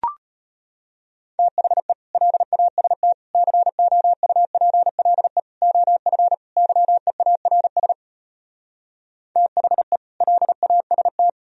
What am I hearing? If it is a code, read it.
Morse code, heard as a keyed tone, about 33 wpm, 700 Hz. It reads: THE LAST COUPLE OF YEARS THE LAST